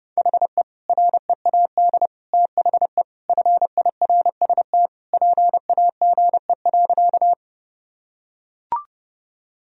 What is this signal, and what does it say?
Morse code, heard as a keyed tone, about 32 wpm, 700 Hz. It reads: HE READ THE FIRST PAGE.
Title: Sentences from Top 1000 Words - Review Part 1 - 30wpm